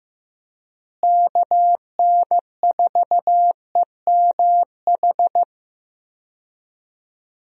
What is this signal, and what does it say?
Morse code, heard as a keyed tone, about 14 wpm, 700 Hz. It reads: KN4EMH
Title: USA Call Signs - 6 Characters - Repeated 3x - 15wpm